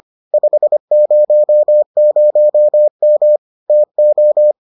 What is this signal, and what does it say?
Morse code, heard as a keyed tone, about 25 wpm, 600 Hz. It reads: 500M TO